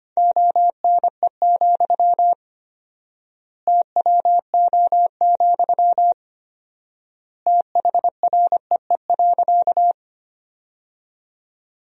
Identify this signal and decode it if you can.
Morse code, heard as a keyed tone, about 25 wpm, 700 Hz. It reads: ONE, TWO, THREE.
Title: 4x Word-Spacing - Sentences from Top 200 Words - 25wpm